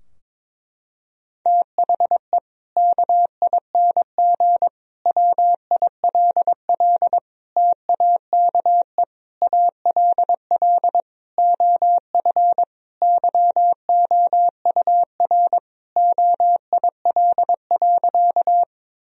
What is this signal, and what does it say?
Morse code, heard as a keyed tone, about 22 wpm, 700 Hz. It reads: THE KING WILL TAKE ALL OF YOUR OIL.
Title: Sentences from Top 1000 Words - Review Part 4 - 22wpm